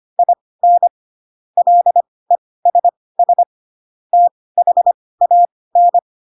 Morse code, heard as a keyed tone, about 25 wpm, spaced as for 18 wpm, 700 Hz. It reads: IN LESS THAN